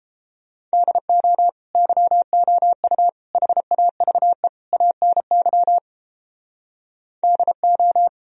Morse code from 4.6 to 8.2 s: ANY DO

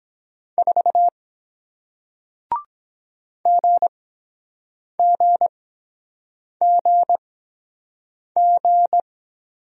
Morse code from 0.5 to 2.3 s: 4

